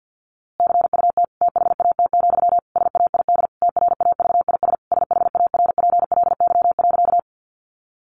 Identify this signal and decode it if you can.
Morse code, heard as a keyed tone, about 50 wpm, 700 Hz. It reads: CUT THAT, SAID TRAVIS SHARPLY.
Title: A Sound Of Thunder 50wpm